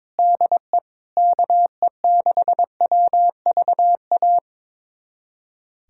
Morse code, heard as a keyed tone, about 22 wpm, 700 Hz. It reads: DE KE6WVA
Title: USA Callsign DE Callsign - 6 Characters - 22wpm